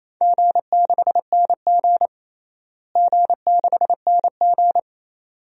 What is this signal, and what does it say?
Morse code, heard as a keyed tone, about 28 wpm, 700 Hz. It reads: G6NG G6NG
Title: DXCC-100 Call Signs - 4 Characters - Repeated 2x - 28wpm